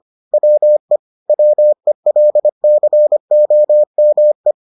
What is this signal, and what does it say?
Morse code, heard as a keyed tone, about 25 wpm, 600 Hz. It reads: WE WELCOME